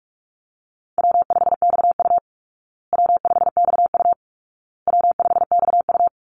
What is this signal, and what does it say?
Morse code, heard as a keyed tone, about 44 wpm, 700 Hz. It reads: W5XU W5XU W5XU
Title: Call Signs Repeated 2x - 45wpm